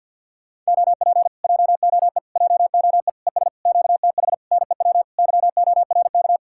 Morse code, heard as a keyed tone, about 63 wpm, 700 Hz. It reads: OP JOE JOE ES QTH NEW YORK